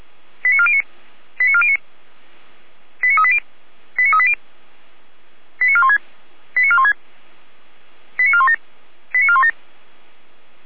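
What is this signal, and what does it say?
Five-tone selective call (ZVEI), heard as quick runs of five tones, tones of about 70 ms. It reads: 89409 89409 89380 89380 89416 89416 89417 89417
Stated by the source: ZVEI five-tone call, address 89500, received on an SDR by feuerwehr-orlamuende (Orlamünde fire brigade).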